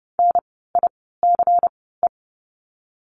Morse code, heard as a keyed tone, about 30 wpm, 700 Hz. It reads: N I C E
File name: Lesson 16f - Letter c - Words 30wpm